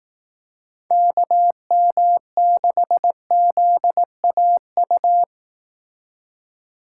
Morse code, heard as a keyed tone, about 18 wpm, 700 Hz. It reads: KM6ZAU
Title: Speed Racing - USA 2x3 Call Signs 15wpm